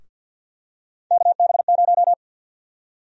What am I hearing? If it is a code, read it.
Morse code, heard as a keyed tone, about 48 wpm, 700 Hz. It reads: KB0
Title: USA Call Sign Prefixes - 50wpm